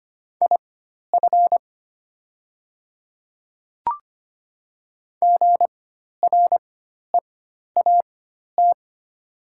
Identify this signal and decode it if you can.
Morse code, heard as a keyed tone, about 25 wpm, spaced as for 12 wpm, 700 Hz. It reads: IF GREAT
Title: 4x Character Spacing - Top 300 Words - 25wpm